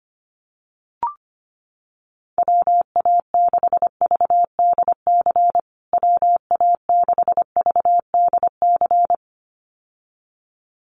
Morse code, heard as a keyed tone, about 25 wpm, 700 Hz. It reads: WA6VDC WA6VDC
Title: USA Call Signs Repeated 2x 25wpm